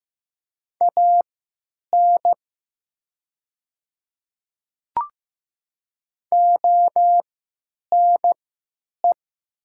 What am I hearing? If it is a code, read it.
Morse code, heard as a keyed tone, about 15 wpm, spaced as for 9 wpm, 700 Hz. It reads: AN ONE
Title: 3x Character Spacing - Top 100 Words - 15wpm